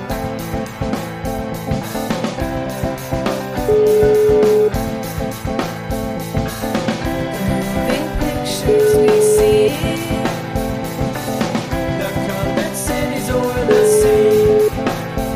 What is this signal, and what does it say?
Ringback tone (Europe, 425 Hz), with noise.